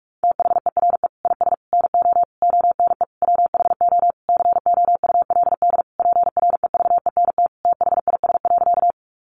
Morse code, heard as a keyed tone, about 44 wpm, 700 Hz. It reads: THERE IS NO ONE WHO COULD PREVENT THIS.
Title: Sentences from Top 1100 Words 45wpm